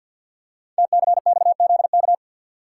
Morse code, heard as a keyed tone, about 48 wpm, 690 Hz. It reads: TCXZK
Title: Code Groups - 5 Letters - 50wpm